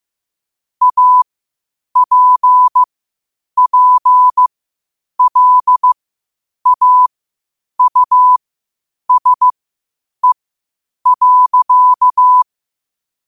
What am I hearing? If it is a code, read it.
Morse code, heard as a keyed tone, about 15 wpm, spaced as for 8 wpm, 1000 Hz. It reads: APPLAUSE.
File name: qotdfiles / 8wpm